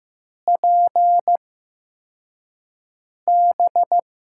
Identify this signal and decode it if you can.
Morse code, heard as a keyed tone, about 15 wpm, spaced as for 4 wpm, 700 Hz. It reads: PB
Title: USA Call Signs - 4 Characters - 8x Character Spacing - 15wpm